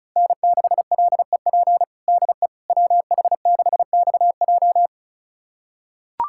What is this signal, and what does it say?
Morse code, heard as a keyed tone, about 36 wpm, 700 Hz. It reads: N6LEP DE WH6XJ
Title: USA Callsign DE Callsign - 5 Characters - 35wpm